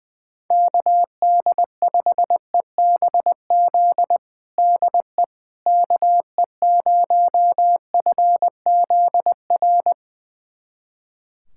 Morse code, heard as a keyed tone, about 20 wpm, 700 Hz. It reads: KD5EBZ DE KE0FZR
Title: USA Callsign DE Callsign - 6 Characters - 20wpm